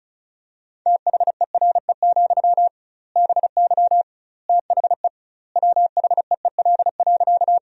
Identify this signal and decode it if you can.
Morse code, heard as a keyed tone, about 36 wpm, 700 Hz. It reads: THERE, BY THE WHEEL.
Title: 2x Word-Spacing - Sentences from Top 500 Words - 35wpm